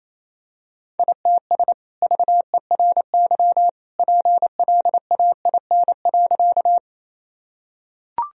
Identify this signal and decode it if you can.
Morse code, heard as a keyed tone, about 28 wpm, 700 Hz. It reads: ITS VERY PLAIN.
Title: Sentences from Top 1000 Words - Review Part 2 - 28wpm